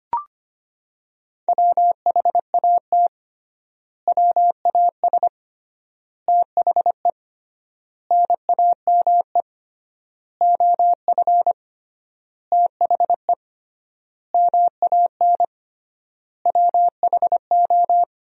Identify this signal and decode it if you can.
Morse code, heard as a keyed tone, about 25 wpm, 700 Hz. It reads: WHAT WAS THE NAME OF THE MAN WHO
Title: 3x Word-Spacing - Sentences from Top 300 Words - 25wpm